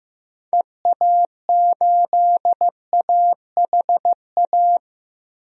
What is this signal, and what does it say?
Morse code, heard as a keyed tone, about 15 wpm, 700 Hz. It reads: EA8AHA
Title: DXCC-100 Call Signs - 6 Characters - Repeated 2x - 15wpm